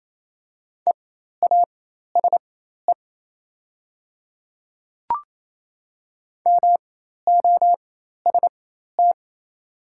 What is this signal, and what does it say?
Morse code, heard as a keyed tone, about 28 wpm, spaced as for 13 wpm, 700 Hz. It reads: EASE MOST
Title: 4x Character Spacing - Top 300 Words - 28wpm